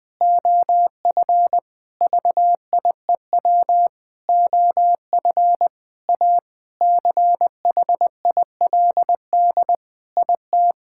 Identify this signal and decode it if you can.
Morse code, heard as a keyed tone, about 21 wpm, 700 Hz. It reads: OF VIEW OF A CHILD IT
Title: Sentences with 5-Word Phrases - 20wpm